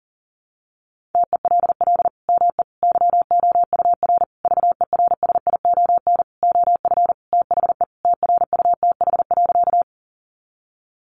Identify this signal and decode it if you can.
Morse code, heard as a keyed tone, about 40 wpm, 700 Hz. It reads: TELL ME YOUR VERSION OF THE TRUTH.
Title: Sentences from Top 1400 Words 40wpm